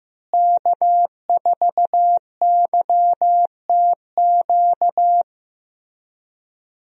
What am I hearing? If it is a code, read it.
Morse code, heard as a keyed tone, about 15 wpm, 700 Hz. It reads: K4YTQ
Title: USA Call Signs - 5 Characters - Repeated 3x - 15wpm